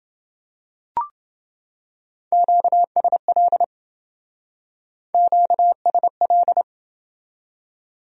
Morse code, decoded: QSL QSL

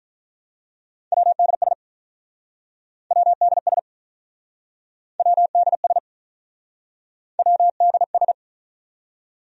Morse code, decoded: WDS WDS WDS WDS

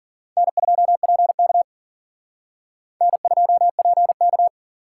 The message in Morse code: N2PK N2PK

